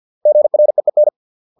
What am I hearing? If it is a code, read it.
Morse code, heard as a keyed tone, about 50 wpm, 600 Hz. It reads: GREEN